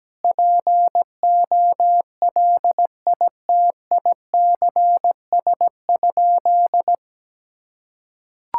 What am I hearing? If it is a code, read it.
Morse code, heard as a keyed tone, about 17 wpm, 700 Hz. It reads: POLITICS?
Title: Sentences from Top 1000 Words - Review Part 6 - 17wpm